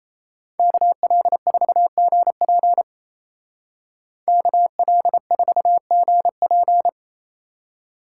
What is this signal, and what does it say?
Morse code, heard as a keyed tone, about 33 wpm, 700 Hz. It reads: KL4GP KL4GP
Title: Speed Racing - USA 2x2 Call Signs 28wpm